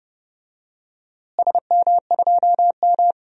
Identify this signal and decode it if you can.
Morse code, heard as a keyed tone, about 31 wpm, 700 Hz. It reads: SM2M
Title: DXCC-100 Call Signs - 4 Characters - Repeated 2x - 30wpm